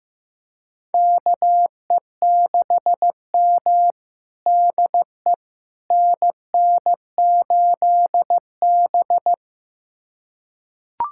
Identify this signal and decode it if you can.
Morse code, heard as a keyed tone, about 15 wpm, 700 Hz. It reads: KE6M DE NN8B
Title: USA Callsign DE Callsign - 4 Characters - 15wpm